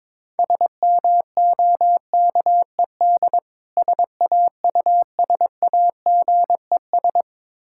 Morse code, heard as a keyed tone, about 22 wpm, 690 Hz. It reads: SMOKED SAUSAGES
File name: Difficult Words with Hints - 22wpm